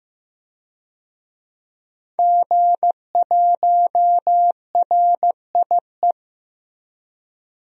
Morse code, heard as a keyed tone, about 15 wpm, 700 Hz. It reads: G1RIE
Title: DXCC-100 Call Signs - 5 Characters - Repeated 3x - 15wpm